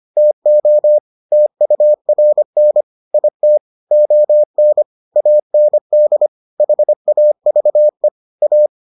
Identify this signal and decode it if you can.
Morse code, heard as a keyed tone, about 25 wpm, 600 Hz. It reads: TO TURN IT ON AND HAVE A